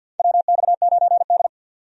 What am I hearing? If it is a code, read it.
Morse code, heard as a keyed tone, about 49 wpm, 700 Hz. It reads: WX9D
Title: USA Call Signs - 4 Characters - 50wpm